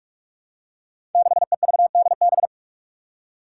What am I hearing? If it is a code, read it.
Morse code, heard as a keyed tone, about 44 wpm, 700 Hz. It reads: 6EVDB
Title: Five Challenging Letter-Number Combinations 45wpm